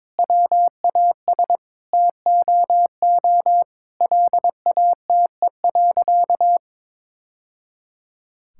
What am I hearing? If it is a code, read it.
Morse code, heard as a keyed tone, about 22 wpm, 700 Hz. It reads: WAS TOO LATE.
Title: Sentences from Top 1000 Words - Review Part 6 - 22wpm